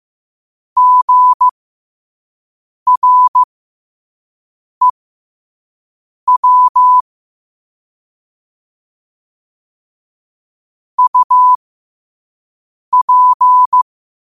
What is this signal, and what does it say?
Morse code, heard as a keyed tone, about 15 wpm, spaced as for 5 wpm, 1000 Hz. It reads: GREW UP
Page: New quotes every day in morse code at 5 Words per minute.